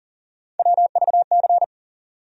Morse code, heard as a keyed tone, about 40 wpm, 700 Hz. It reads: WVC
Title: USA Call Sign Postfixes - 3 Letters 40wpm